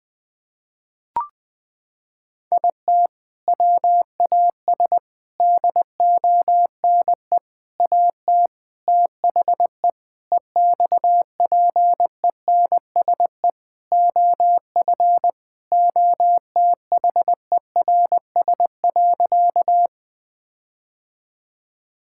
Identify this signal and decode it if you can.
Morse code, heard as a keyed tone, about 20 wpm, 700 Hz. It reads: IT WAS DONE AT THE EXPENSE OF OTHERS.
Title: Sentences with 4-Word Phrases - 20wpm